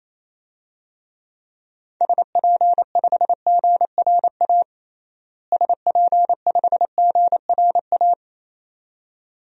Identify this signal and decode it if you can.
Morse code, heard as a keyed tone, about 28 wpm, 700 Hz. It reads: SP5GRA SP5GRA
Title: DXCC-100 Call Signs - 6 Characters - Repeated 2x - 28wpm